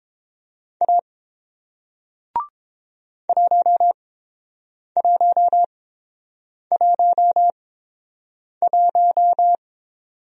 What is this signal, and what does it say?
Morse code, heard as a keyed tone, about 30 wpm, spaced as for 7 wpm, 690 Hz. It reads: A 1111